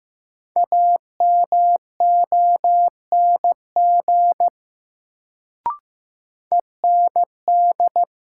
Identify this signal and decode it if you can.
Morse code, heard as a keyed tone, about 15 wpm, 700 Hz. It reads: AMONG END
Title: Sets of 2 Words - New Word Drill - 15wpm